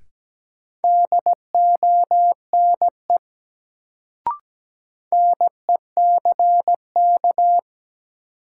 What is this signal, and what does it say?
Morse code, heard as a keyed tone, about 17 wpm, 700 Hz. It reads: DONE NECK
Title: Sets of 5 Words Encore - New Word Drill - 17wpm